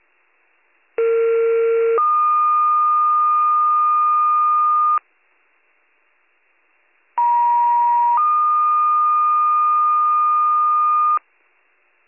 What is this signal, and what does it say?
Two-tone page, heard as a one-second tone, then a three-second tone, about 460 then 1220 Hz, about 950 then 1220 Hz. It is heard from 1.0 to 11.2 s.